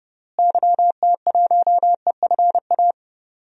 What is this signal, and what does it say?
Morse code, heard as a keyed tone, about 32 wpm, 700 Hz. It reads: YT1EFA
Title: DXCC-100 Call Signs - 6 Characters - Repeated 2x - 30wpm